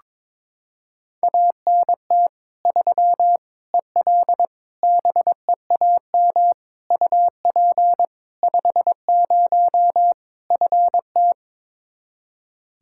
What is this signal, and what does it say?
Morse code, heard as a keyed tone, about 22 wpm, 700 Hz. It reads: ANT 3 EL BEAM UP 50 FT
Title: Practice QSO - Exchange 2 - Info - 22wpm